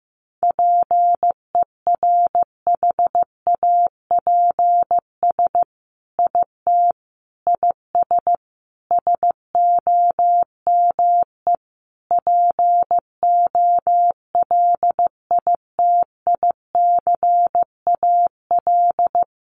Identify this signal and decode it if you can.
Morse code, heard as a keyed tone, about 15 wpm, 700 Hz. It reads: PERHAPS IT IS SOME POLITICAL
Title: Sentences from Top 800 Words 15wpm